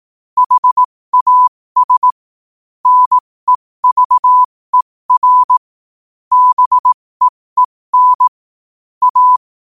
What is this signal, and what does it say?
Morse code, heard as a keyed tone, about 18 wpm, spaced as for 15 wpm, 1000 Hz. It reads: HAS NEVER BEEN A